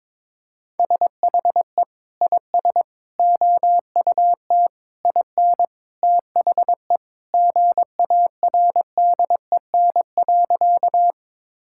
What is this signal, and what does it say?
Morse code, heard as a keyed tone, about 22 wpm, 700 Hz. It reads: SHE IS OUT IN THE GARDEN.
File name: practice / 22wpm / Sentences with 3-Word Phrases - 22wpm